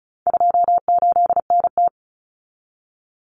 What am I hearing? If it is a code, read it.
Morse code, heard as a keyed tone, about 34 wpm, 700 Hz. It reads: 28NT
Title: Common Cut Numbers - 4-Digit 35wpm